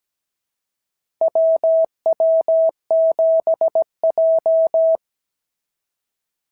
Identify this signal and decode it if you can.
Morse code, heard as a keyed tone, about 17 wpm, 650 Hz. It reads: WW7J
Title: SOTA QSO Practice 17wpm